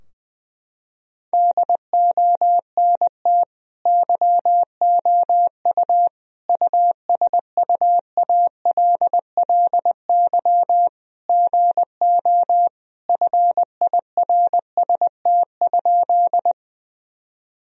Morse code, heard as a keyed tone, about 20 wpm, 700 Hz. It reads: DONT YOU USUALLY GO FIRST?
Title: Sentences from Top 1000 Words - Review Part 3 - 20wpm